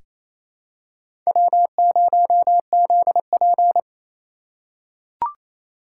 Morse code, heard as a keyed tone, about 28 wpm, 700 Hz. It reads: W0ZP
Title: K1USN SST QSO Practice - 28wpm